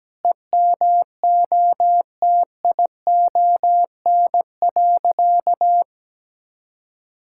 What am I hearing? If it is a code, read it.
Morse code, heard as a keyed tone, about 17 wpm, 700 Hz. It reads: EMOTION.
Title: Sentences from Top 1600 Words 17wpm